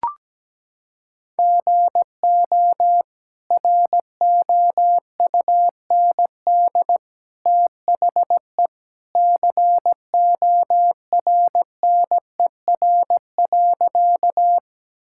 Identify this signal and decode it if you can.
Morse code, heard as a keyed tone, about 17 wpm, 700 Hz. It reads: GO ROUND THE CORNER.
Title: Sentences from Top 1600 Words 17wpm